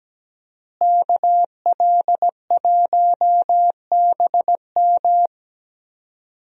Morse code, heard as a keyed tone, about 17 wpm, 700 Hz. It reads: KL1BM
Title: USA Call Signs - 5 Characters - Repeated 3x - 17wpm